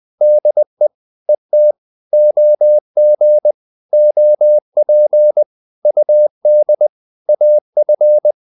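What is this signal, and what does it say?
Morse code, heard as a keyed tone, about 20 wpm, 600 Hz. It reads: DE ET OG OP UD AF
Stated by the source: Morse code practice of short Danish words at 20 wpm, from CW med Gnister.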